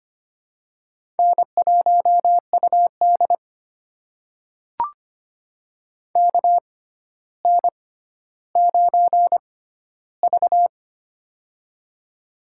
Morse code, heard as a keyed tone, about 25 wpm, spaced as for 13 wpm, 700 Hz. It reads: N1UD KN9V
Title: USA Call Signs - 4 Characters - 6x Character Spacing - 25wpm